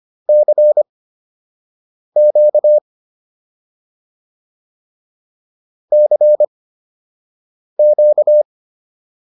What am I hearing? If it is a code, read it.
Morse code, heard as a keyed tone, about 25 wpm, spaced as for 6 wpm, 600 Hz. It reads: CQ CQ